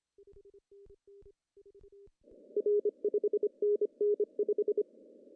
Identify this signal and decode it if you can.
Morse code, heard as a keyed tone, about 25 wpm, 420 Hz. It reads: R 5NN 5
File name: CW / cqww-cw-k5tr